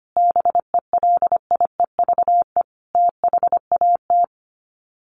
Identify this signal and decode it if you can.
Morse code, heard as a keyed tone, about 25 wpm, 700 Hz. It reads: BELIEVE THAT
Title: Sets of 5 Words Encore2 25wpm